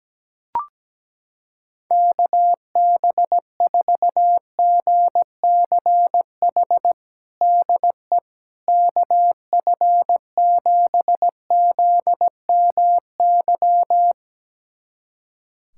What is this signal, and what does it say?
Morse code, heard as a keyed tone, about 17 wpm, 700 Hz. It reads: KB4GCH DE KF7ZMY
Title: USA Callsign DE Callsign 17wpm